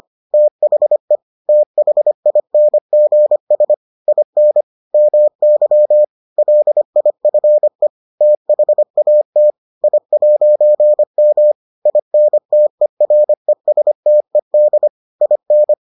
Morse code, heard as a keyed tone, about 25 wpm, 600 Hz. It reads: THE THINGS IN MY LIFE THAT I'M INTERESTED IN